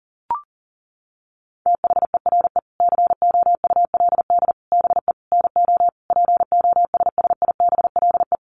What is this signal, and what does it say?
Morse code, heard as a keyed tone, about 40 wpm, 700 Hz. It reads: THERE COULD BE NO POSSIBLE
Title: Sentences from Top 700 Words 40wpm